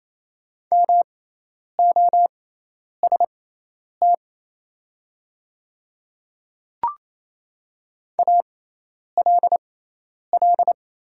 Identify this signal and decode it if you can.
Morse code, heard as a keyed tone, about 27 wpm, spaced as for 10 wpm, 690 Hz. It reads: MOST ALL
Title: 6x Character Spacing - Top 100 Words - 28wpm